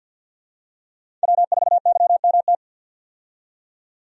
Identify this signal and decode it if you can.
Morse code, heard as a keyed tone, about 50 wpm, 700 Hz. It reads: W4YMT
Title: USA Call Signs - 5 Characters - 50wpm